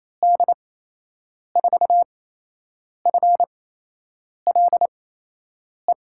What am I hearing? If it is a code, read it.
Morse code, heard as a keyed tone, about 27 wpm, spaced as for 8 wpm, 700 Hz. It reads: D4FLE
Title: USA Call Signs - 6 Characters - 8x Character Spacing - 28wpm